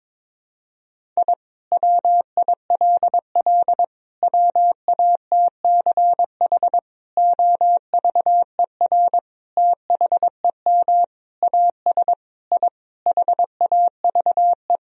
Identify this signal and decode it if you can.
Morse code, heard as a keyed tone, about 22 wpm, 700 Hz. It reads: I WILL WATCH OVER THEM AS I HAVE
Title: Sentences from Top 1000 Words - Review Part 1 - 22wpm